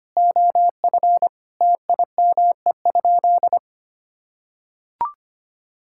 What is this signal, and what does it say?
Morse code, heard as a keyed tone, about 25 wpm, 700 Hz. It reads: OF TIME?
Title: Sentences from Top 1000 Words - Review Part 5 - 25wpm